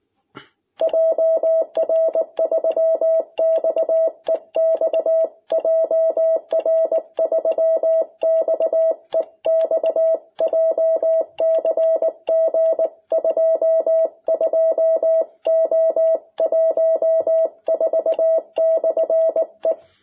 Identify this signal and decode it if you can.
Morse code, heard as a keyed tone, about 19 wpm, 660 Hz. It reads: JR3XEX JR3XEX JCG 22O14/E
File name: jh8cbh / 2021cw